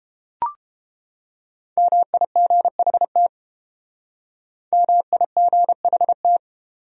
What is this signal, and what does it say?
Morse code, heard as a keyed tone, about 29 wpm, 700 Hz. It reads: MIGHT MIGHT